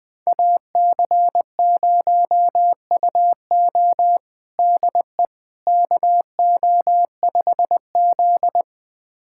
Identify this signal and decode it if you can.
Morse code, heard as a keyed tone, about 20 wpm, 700 Hz. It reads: AC0UO DE KO5Z
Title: USA Callsign DE Callsign 20wpm